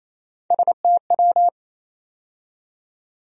Morse code, heard as a keyed tone, about 28 wpm, 700 Hz. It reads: STW
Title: USA Call Sign Postfixes - 3 Letters 28wpm